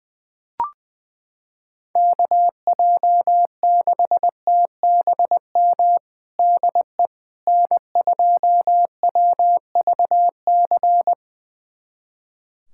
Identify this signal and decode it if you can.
Morse code, heard as a keyed tone, about 21 wpm, 700 Hz. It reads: KJ6TBM DE N2WVC
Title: USA Callsign DE Callsign 20wpm